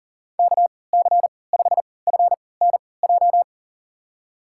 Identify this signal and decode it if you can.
Morse code, heard as a keyed tone, about 40 wpm, spaced as for 23 wpm, 700 Hz. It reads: KC5FNJ
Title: USA Call Signs - 6 Characters - 3x Character Spacing - 40wpm